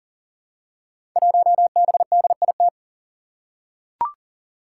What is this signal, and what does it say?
Morse code, heard as a keyed tone, about 40 wpm, 700 Hz. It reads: 1BDIT
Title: Five Challenging Letter-Number Combinations 40wpm